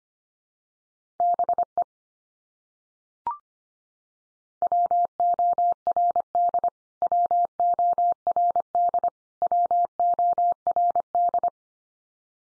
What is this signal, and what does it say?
Morse code, heard as a keyed tone, about 25 wpm, 700 Hz. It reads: BE WORD WORD WORD
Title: Top 100 Words 25wpm x3